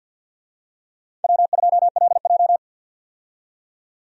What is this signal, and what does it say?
Morse code, heard as a keyed tone, about 51 wpm, 700 Hz. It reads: W2LJ
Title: USA Call Signs - 4 Characters - 50wpm